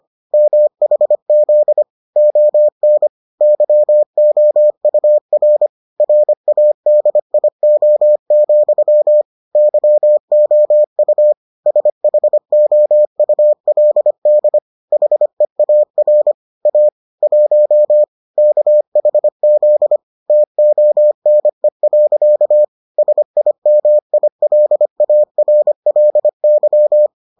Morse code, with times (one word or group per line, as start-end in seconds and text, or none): 0.3-1.8 MHZ
2.2-3.1 ON
3.4-5.7 YOUR
6.0-9.2 RADIO,
9.6-11.3 YOU
11.7-14.6 SHOULD
14.9-16.3 HEAR
16.7-16.9 A
17.2-18.0 1
18.4-20.0 KHZ
20.3-22.6 TONE.
23.0-27.1 SIMILARLY